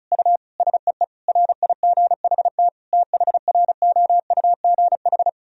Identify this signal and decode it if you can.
Morse code, heard as a keyed tone, about 34 wpm, 700 Hz. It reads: U SEE RIGHT THROUGH
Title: Sentences from Top 200 Words and Abbreviations 35wpm